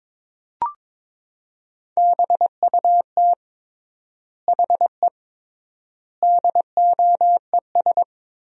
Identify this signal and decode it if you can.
Morse code, heard as a keyed tone, about 22 wpm, 700 Hz. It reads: BUT HE DOES
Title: 3x Word-Spacing - Sentences from Top 200 Words - 22wpm